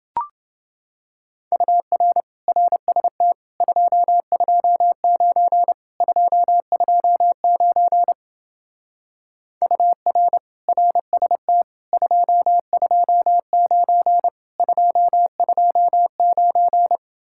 Morse code, heard as a keyed tone, about 28 wpm, 700 Hz. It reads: UR RST 229 229 UR RST 229 229